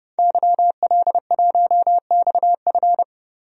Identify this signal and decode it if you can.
Morse code, heard as a keyed tone, about 30 wpm, 700 Hz. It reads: YL1XF
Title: DXCC-100 Call Signs - 5 Characters - Repeated 3x - 30wpm